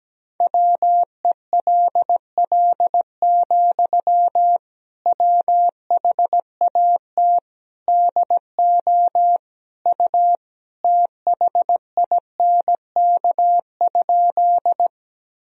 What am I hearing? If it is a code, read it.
Morse code, heard as a keyed tone, about 17 wpm, 700 Hz. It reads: WELL, WHAT DO U THINK?
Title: Sentences from Top 200 Words and Abbreviations 17wpm